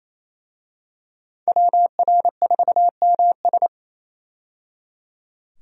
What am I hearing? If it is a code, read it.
Morse code, heard as a keyed tone, about 27 wpm, 700 Hz. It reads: WR4MS
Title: USA Call Signs - 5 Characters - 28wpm